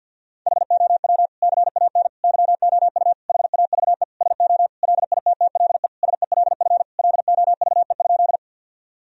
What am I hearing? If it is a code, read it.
Morse code, heard as a keyed tone, about 49 wpm, 700 Hz. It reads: HOW CAN YOU HAVE SO LITTLE SELF LOVE?
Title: Sentences from Top 1000 Words - Review Part 1 - 50wpm